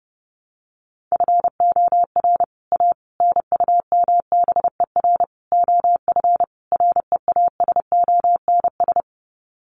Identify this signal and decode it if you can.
Morse code, heard as a keyed tone, about 30 wpm, 700 Hz. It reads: FOR A NUMBER OF REASONS
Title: Sets of 5 Words Encore2 30wpm